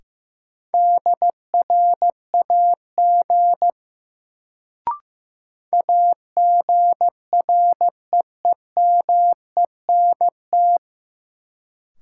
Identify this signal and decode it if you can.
Morse code, heard as a keyed tone, about 15 wpm, 700 Hz. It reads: DRAG AGREEMENT
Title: Sentences from Top 1500 Words - New Word Drill - 15wpm